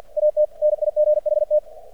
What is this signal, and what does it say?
Morse code, heard as a keyed tone, about 50 wpm, 600 Hz. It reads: AT NIGHT
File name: archived / CRSnet